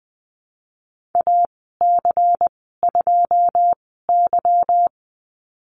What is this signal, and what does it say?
Morse code, heard as a keyed tone, about 20 wpm, spaced as for 15 wpm, 700 Hz. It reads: AC2Y
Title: 186 Lesson 30g - Letter b - Call Signs 20wpm